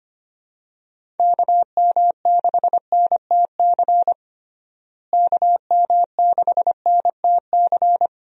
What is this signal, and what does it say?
Morse code, heard as a keyed tone, about 25 wpm, 700 Hz. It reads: KM6NTC KM6NTC
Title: USA Call Signs - 6 Characters - Repeated 3x - 25wpm